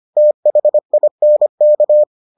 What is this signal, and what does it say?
Morse code, heard as a keyed tone, about 25 wpm, 600 Hz. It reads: THINK